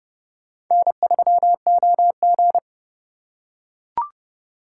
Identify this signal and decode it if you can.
Morse code, heard as a keyed tone, about 28 wpm, 700 Hz. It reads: N3OG